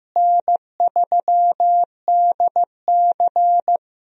Morse code, heard as a keyed tone, about 15 wpm, 700 Hz. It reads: N3DC
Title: USA Call Signs - 4 Characters - Repeated 2x - 15wpm